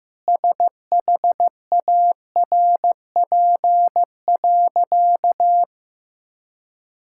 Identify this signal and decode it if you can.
Morse code, heard as a keyed tone, about 16 wpm, 700 Hz. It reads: SHARP.
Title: Sentences from Top 1600 Words 15wpm